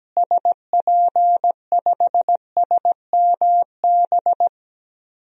Morse code, heard as a keyed tone, about 17 wpm, 700 Hz. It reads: SP5SMB
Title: DXCC-100 Call Signs - 6 Characters - Repeated 2x - 17wpm